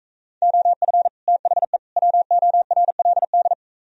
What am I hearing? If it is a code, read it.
Morse code, heard as a keyed tone, about 45 wpm, 700 Hz. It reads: OF THE WORLD